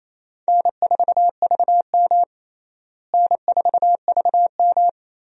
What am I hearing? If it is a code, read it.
Morse code, heard as a keyed tone, about 28 wpm, 700 Hz. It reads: N4VM N4VM
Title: USA Call Signs - 4 Characters - Repeated 3x - 28wpm